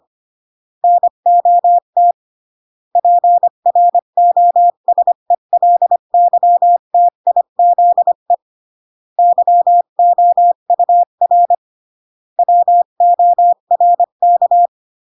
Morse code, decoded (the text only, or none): NOT PROSELYTIZE YOUR WORK